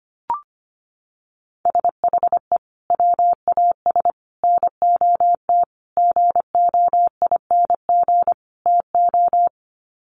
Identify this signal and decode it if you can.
Morse code, heard as a keyed tone, about 25 wpm, 700 Hz. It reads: SHE WAS NOT GOING TO
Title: Sets of 5 Words Encore2 25wpm